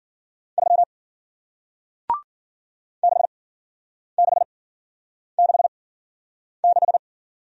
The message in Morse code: 3 6666

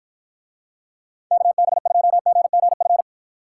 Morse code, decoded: KB2GGF

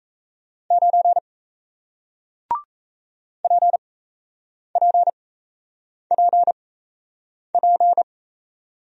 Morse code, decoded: 9 PPPP

